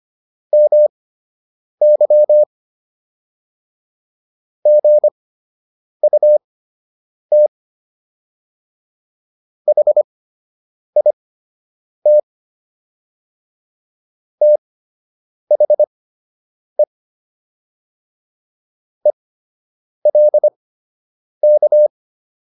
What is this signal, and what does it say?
Morse code, heard as a keyed tone, about 25 wpm, spaced as for 8 wpm, 600 Hz. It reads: MY GUT HIT THE ELK